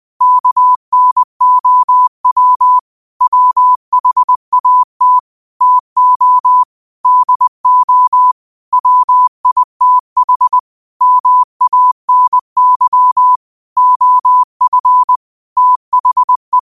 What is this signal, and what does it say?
Morse code, heard as a keyed tone, about 20 wpm, 1000 Hz. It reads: KNOW WHAT TO DO WITH MANY OF THE